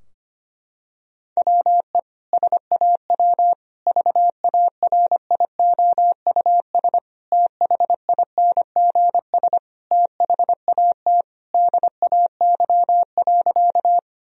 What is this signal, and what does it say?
Morse code, heard as a keyed tone, about 25 wpm, 700 Hz. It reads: WE SAW VARIOUS THINGS THAT DAY.
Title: Sentences from Top 1000 Words - Review Part 4 - 25wpm